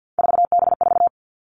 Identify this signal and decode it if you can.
Morse code, heard as a keyed tone, about 50 wpm, 700 Hz. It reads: 4B4
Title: Three Challenging Letter-Number Combinations 50wpm